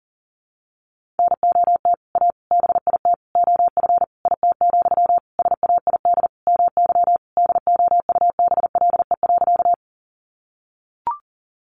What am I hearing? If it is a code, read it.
Morse code, heard as a keyed tone, about 40 wpm, 700 Hz. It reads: NOT A BIT OF IT, SAID MY DOUBLE.
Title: Sentences from Top 1400 Words 40wpm